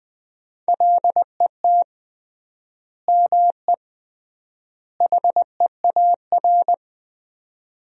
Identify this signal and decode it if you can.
Morse code, heard as a keyed tone, about 20 wpm, 700 Hz. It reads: LET ME HEAR
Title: 3x Word-Spacing - Sentences from Top 300 Words - 20wpm